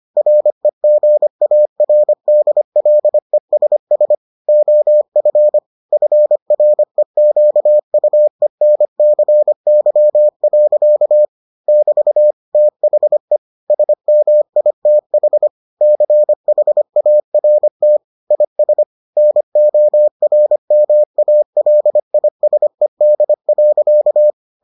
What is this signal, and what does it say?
Morse code, heard as a keyed tone, about 25 wpm, 600 Hz. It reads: REGARDLESS OF FREQUENCY. <BT> THE SMITH CHART IS NORMALISED.